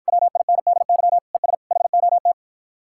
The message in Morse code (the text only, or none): WINDY ES HOT